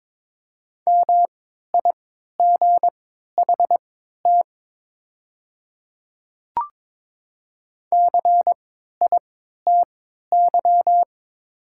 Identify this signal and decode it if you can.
Morse code, heard as a keyed tone, about 22 wpm, spaced as for 12 wpm, 700 Hz. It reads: MIGHT CITY
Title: 3x Character Spacing - Top 300 Words - 22wpm